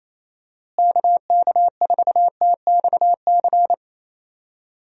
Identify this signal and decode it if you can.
Morse code, heard as a keyed tone, about 28 wpm, 700 Hz. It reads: KK4TXC
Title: USA Call Signs - 6 Characters - Repeated 3x - 28wpm